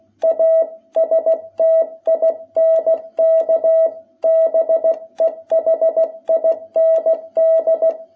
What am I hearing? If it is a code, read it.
Morse code, heard as a keyed tone, about 16 wpm, 660 Hz. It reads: A STINK BEHIND
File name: jh8cbh / 2021cw